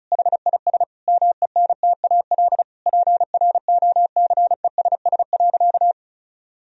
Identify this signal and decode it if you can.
Morse code, heard as a keyed tone, about 36 wpm, 700 Hz. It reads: HIS MENTAL PROCESS.
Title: Sentences from Top 1000 Words - Review Part 4 - 35wpm